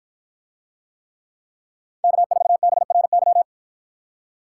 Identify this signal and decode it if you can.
Morse code, heard as a keyed tone, about 53 wpm, 700 Hz. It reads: K4BRY